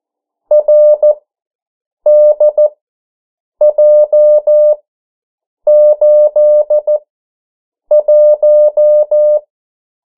Morse code, heard as a keyed tone, about 14 wpm, spaced as for 7 wpm, 600 Hz. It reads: RDJ81
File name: morse mp3 / 09 urok 8